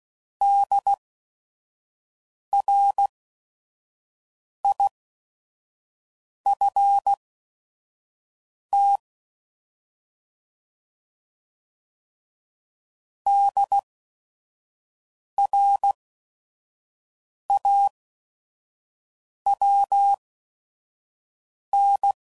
Morse code, heard as a keyed tone, about 16 wpm, spaced as for 5 wpm, 800 Hz. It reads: DRIFT DRAWN
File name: Prefixes / By First Letter / Ds